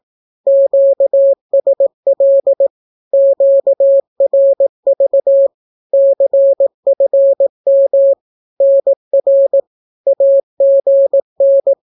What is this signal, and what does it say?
Morse code, heard as a keyed tone, about 18 wpm, 550 Hz. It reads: QSL QRV CFM NR AGN
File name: naukapodstaw / skroty / skroty